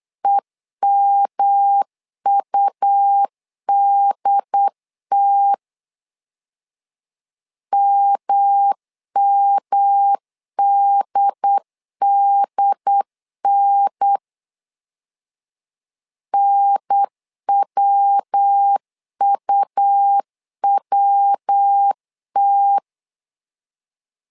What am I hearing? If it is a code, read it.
Morse code, heard as a keyed tone, about 8 wpm, 800 Hz. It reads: EMUDT MMDDN NWUWT
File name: LEZIONI / 3 MWRO / 3 MW 8wpm 2501